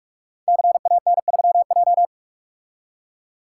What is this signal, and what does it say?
Morse code, heard as a keyed tone, about 44 wpm, 700 Hz. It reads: CAN3J
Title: Code Groups - 5 Letter:Numbers - 45wpm